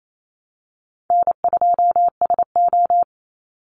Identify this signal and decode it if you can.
Morse code, heard as a keyed tone, about 28 wpm, 700 Hz. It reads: N2SO
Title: Easy Call Signs Repeated 2x - 28wpm